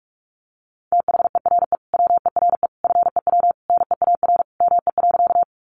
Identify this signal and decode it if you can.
Morse code, heard as a keyed tone, about 44 wpm, 700 Hz. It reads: THERE WERE FEW NEAR ME.
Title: Sentences from Top 300 Words 45wpm